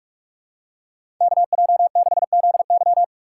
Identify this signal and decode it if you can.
Morse code, heard as a keyed tone, about 46 wpm, 700 Hz. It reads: KJ6ZY
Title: USA Call Signs - 5 Characters - 3x Character Spacing - 45wpm